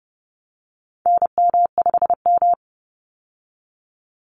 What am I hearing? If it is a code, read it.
Morse code, heard as a keyed tone, about 30 wpm, 700 Hz. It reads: NM5M
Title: Call Signs Repeated 2x - 30wpm